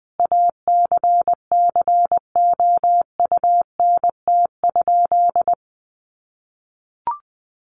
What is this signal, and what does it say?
Morse code, heard as a keyed tone, about 20 wpm, 700 Hz. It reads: ACCOUNT?
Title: Sentences from Top 1500 Words 20wpm